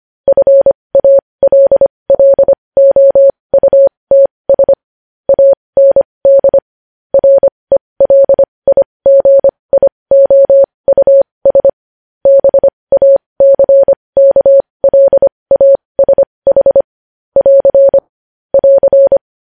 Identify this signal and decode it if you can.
Morse code, heard as a keyed tone, about 25 wpm, spaced as for 20 wpm, 550 Hz. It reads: FALLOUTS AND RELIGIOUS BACKLASH <AR> <AR>
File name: podcast / News headlines-1x-25x20